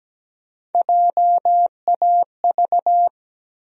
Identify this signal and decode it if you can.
Morse code, heard as a keyed tone, about 17 wpm, 700 Hz. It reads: JAV
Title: USA Call Sign Postfixes - 3 Letters 17wpm